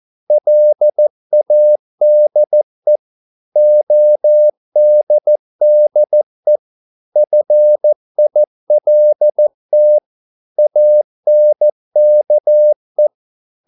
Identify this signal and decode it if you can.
Morse code, heard as a keyed tone, about 14 wpm, 600 Hz. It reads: LADE ODDE FILT ANKE
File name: Korte ord DK 14wpm